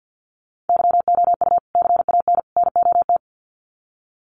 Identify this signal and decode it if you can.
Morse code, heard as a keyed tone, about 50 wpm, 700 Hz. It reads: YOU CAN NOT
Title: Sets of 3 Words 50wpm